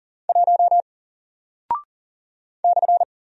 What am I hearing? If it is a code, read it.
Morse code, heard as a keyed tone, about 40 wpm, 700 Hz. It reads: 1 /
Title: Single Character (Letter:Number:Punctuation:Prosign) Alt 40wpm